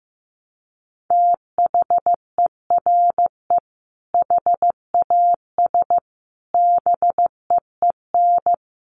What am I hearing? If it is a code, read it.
Morse code, heard as a keyed tone, about 15 wpm, 700 Hz. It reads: THERE HAS BEEN
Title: Sets of 4 Words 15wpm